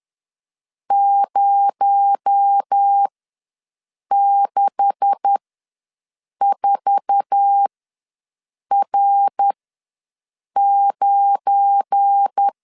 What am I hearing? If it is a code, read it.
Morse code, heard as a keyed tone, about 10 wpm, spaced as for 6 wpm, 800 Hz. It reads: 064R9
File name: LEZIONI / 9 286 ws 10wpm 4032